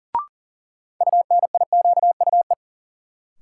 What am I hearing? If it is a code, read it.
Morse code, heard as a keyed tone, about 37 wpm, 700 Hz. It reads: UNIQUE
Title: Sentences from Top 1000 Words - Review Part 7 - New Word Drill - 40wpm